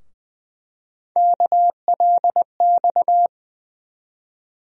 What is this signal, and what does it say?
Morse code, heard as a keyed tone, about 21 wpm, 700 Hz. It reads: KLX
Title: USA Call Sign Postfixes - 3 Letters 20wpm